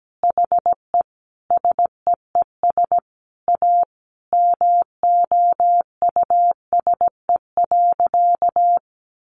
Morse code, heard as a keyed tone, about 17 wpm, 700 Hz. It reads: HE SEES A MOUSE.
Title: Easy Phrases 17wpm